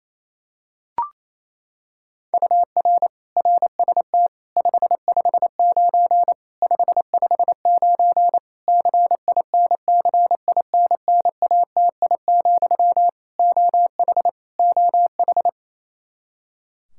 Morse code, heard as a keyed tone, about 28 wpm, 700 Hz. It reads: UR RST 559 559 CINCINNATI, OH OH
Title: Practice QSO - Exchange 1 - Beginning - 28wpm